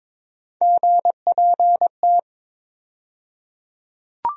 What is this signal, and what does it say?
Morse code, heard as a keyed tone, about 22 wpm, 700 Hz. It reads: GPT
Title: USA Call Sign Postfixes - 3 Letters 22wpm